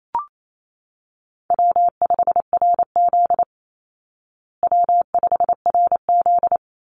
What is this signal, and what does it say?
Morse code, heard as a keyed tone, about 28 wpm, 700 Hz. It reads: W5RZ W5RZ
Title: Easy Call Signs Repeated 2x - 28wpm